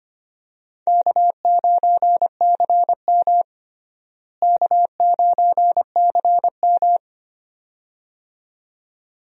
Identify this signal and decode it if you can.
Morse code, heard as a keyed tone, about 25 wpm, 700 Hz. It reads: K9CM K9CM
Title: USA Call Signs - 4 Characters - Repeated 2x - 25wpm